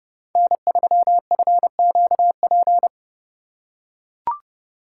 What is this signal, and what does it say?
Morse code, heard as a keyed tone, about 31 wpm, 700 Hz. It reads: N3FQP
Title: USA Call Signs - 5 Characters - 2x Character Spacing - 30wpm